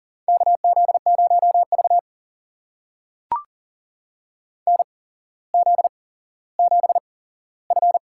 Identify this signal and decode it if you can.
Morse code, heard as a keyed tone, about 40 wpm, spaced as for 17 wpm, 700 Hz. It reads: KZ0V NZ7F
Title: USA Call Signs - 4 Characters - 8x Character Spacing - 40wpm